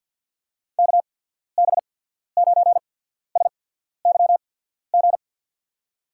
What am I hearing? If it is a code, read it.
Morse code, heard as a keyed tone, about 54 wpm, spaced as for 14 wpm, 700 Hz. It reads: KB9SYG
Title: USA Call Signs - 6 Characters - 8x Character Spacing - 50wpm